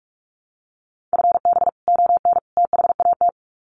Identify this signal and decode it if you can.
Morse code, heard as a keyed tone, about 44 wpm, 700 Hz. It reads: FB ON THAT
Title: Common QSO Phrases 45wpm